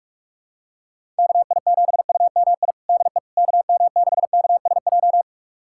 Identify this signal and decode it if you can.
Morse code, heard as a keyed tone, about 46 wpm, 700 Hz. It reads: KI7UMI DE KM6KSJ
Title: USA Callsign DE Callsign - 6 Characters - 45wpm